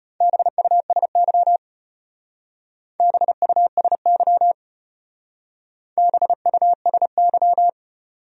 Morse code, heard as a keyed tone, about 34 wpm, 700 Hz. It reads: BUSY BUSY BUSY